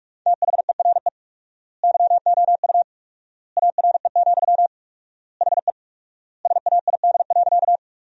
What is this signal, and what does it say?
Morse code, heard as a keyed tone, about 46 wpm, 700 Hz. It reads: THERE YOU ARE, HE SAID.
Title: 4x Word-Spacing - Sentences from Top 100 Words - 45wpm